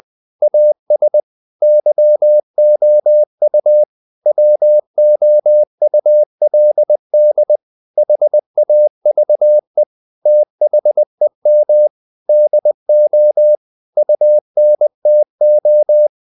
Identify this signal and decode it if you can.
Morse code, heard as a keyed tone, about 20 wpm, 600 Hz. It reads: AS YOU WOULD HAVE THEM DO UNTO
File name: CW / Practice / EnglishSayings20